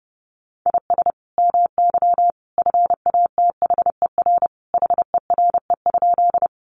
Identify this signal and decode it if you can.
Morse code, heard as a keyed tone, about 30 wpm, 700 Hz. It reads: IS MY FATHER HERE?
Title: Sentences from Top 300 Words 30wpm